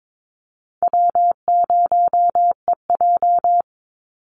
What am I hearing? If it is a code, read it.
Morse code, heard as a keyed tone, about 22 wpm, 700 Hz. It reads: W0EJ
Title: Easy Call Signs - 22wpm